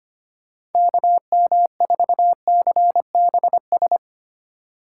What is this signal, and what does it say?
Morse code, heard as a keyed tone, about 30 wpm, 700 Hz. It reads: KM4CBS
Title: Speed Racing - USA 2x3 Call Signs 25wpm